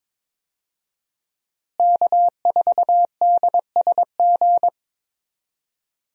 Morse code, heard as a keyed tone, about 22 wpm, 700 Hz. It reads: K4DSG
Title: USA Call Signs - 5 Characters - Repeated 2x - 22wpm